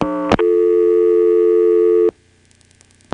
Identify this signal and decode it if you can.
Dial tone (North America), 350 + 440 Hz.